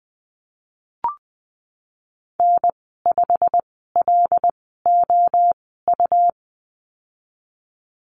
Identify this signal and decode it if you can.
Morse code, heard as a keyed tone, about 20 wpm, spaced as for 15 wpm, 700 Hz. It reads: N5LOU
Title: 179 Lesson 29g - Slash - Call Signs 20wpm